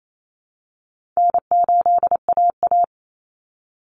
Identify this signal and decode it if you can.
Morse code, heard as a keyed tone, about 28 wpm, 700 Hz. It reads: N8AA
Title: Call Signs Repeated 3x - 28wpm